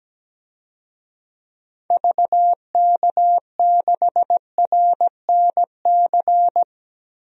Morse code, heard as a keyed tone, about 17 wpm, 700 Hz. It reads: VK6RNC
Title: DXCC-100 Call Signs - 6 Characters - Repeated 3x - 17wpm